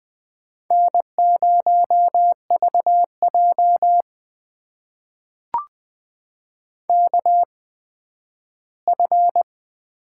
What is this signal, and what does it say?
Morse code, heard as a keyed tone, about 20 wpm, spaced as for 12 wpm, 700 Hz. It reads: N0VJ KF
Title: USA Call Signs - 8x Character Spacing - 20wpm